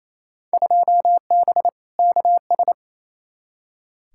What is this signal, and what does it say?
Morse code, heard as a keyed tone, about 28 wpm, 700 Hz. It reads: 2B KS
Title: Field Day Exchanges Encore - 28wpm